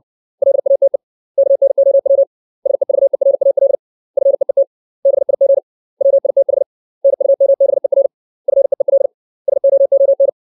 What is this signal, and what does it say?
Morse code, heard as a keyed tone, about 60 wpm, 550 Hz. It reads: LATE KNOW SEVERAL FEET BIG WITH NUMBER FEEL SOON
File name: 632 words / 632 Words 60WPM